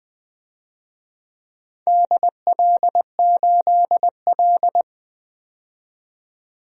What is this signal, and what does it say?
Morse code, heard as a keyed tone, about 20 wpm, 700 Hz. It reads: DL8L
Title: DXCC-100 Call Signs - 4 Characters - 20wpm